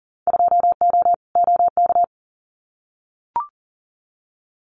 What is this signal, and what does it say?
Morse code, heard as a keyed tone, about 40 wpm, 700 Hz. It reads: 2O OK
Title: Winter Field Day Exchanges 40wpm